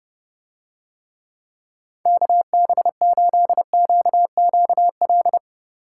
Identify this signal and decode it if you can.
Morse code, heard as a keyed tone, about 30 wpm, 700 Hz. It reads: KB8QQL